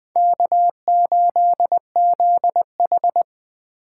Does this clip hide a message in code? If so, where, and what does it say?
Yes: Morse code from 0.2 to 3.2 s, heard as a keyed tone, about 20 wpm, 700 Hz. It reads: K8ZH